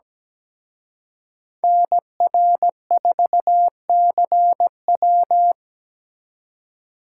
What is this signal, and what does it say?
Morse code, heard as a keyed tone, about 20 wpm, 700 Hz. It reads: NR4CW
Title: Speed Racing - USA 2x2 Call Signs 17wpm